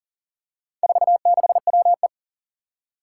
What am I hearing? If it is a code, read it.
Morse code, heard as a keyed tone, about 40 wpm, 700 Hz. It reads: 46WE
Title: Four Challenging Letter-Number Combinations 40wpm